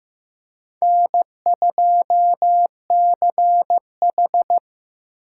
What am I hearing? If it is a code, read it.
Morse code, heard as a keyed tone, about 18 wpm, 700 Hz. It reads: N2CH
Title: Speed Racing - USA 1x2 Call Signs 15wpm